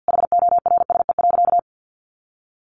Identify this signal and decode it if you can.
Morse code, heard as a keyed tone, about 50 wpm, 700 Hz. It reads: HORSE.
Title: Sentences from Top 500 Words 50wpm